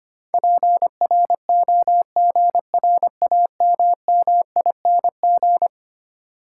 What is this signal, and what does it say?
Morse code, heard as a keyed tone, about 25 wpm, 700 Hz. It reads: PROGRAMMING
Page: morse-code-ninja-practice-25wpm